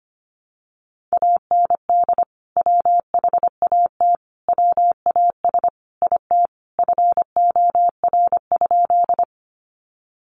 Morse code, heard as a keyed tone, about 25 wpm, 700 Hz. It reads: AND WHAT WAS IT FOR?
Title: Sentences from Top 100 Words 25wpm